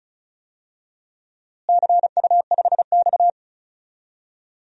Morse code, heard as a keyed tone, about 36 wpm, 700 Hz. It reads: CU5X
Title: DXCC-100 Call Signs - 4 Characters - Repeated 2x - 35wpm